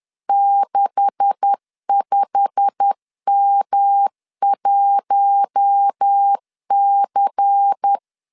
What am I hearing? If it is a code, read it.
Morse code, heard as a keyed tone, about 10 wpm, 800 Hz. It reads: 65M1C
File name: CW / DET / 1 9 1500C S B 10wpm 4439